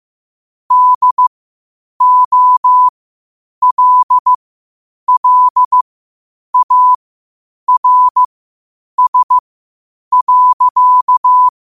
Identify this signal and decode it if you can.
Morse code, heard as a keyed tone, about 15 wpm, spaced as for 8 wpm, 1000 Hz. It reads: DOLLARS.
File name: qotdfiles / 8wpm